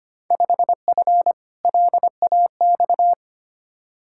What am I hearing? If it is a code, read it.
Morse code, heard as a keyed tone, about 25 wpm, 700 Hz. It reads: 5F LAX
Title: Field Day Exchanges Encore - 25wpm